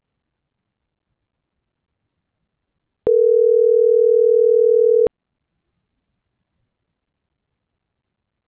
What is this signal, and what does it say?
Ringback tone (North America), 440 + 480 Hz.